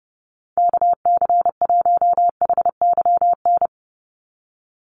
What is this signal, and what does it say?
Morse code, heard as a keyed tone, about 29 wpm, 700 Hz. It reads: KC1HYN
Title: USA Call Signs 30wpm